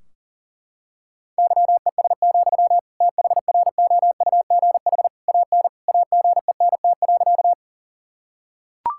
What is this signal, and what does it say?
Morse code, heard as a keyed tone, about 40 wpm, 700 Hz. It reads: YES, THROUGH AN AGENT.
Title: Sentences from Top 1600 Words 40wpm